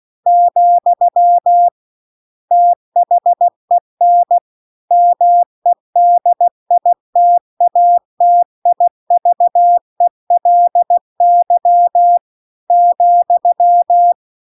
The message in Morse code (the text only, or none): , THEN MEDITATIVELY ,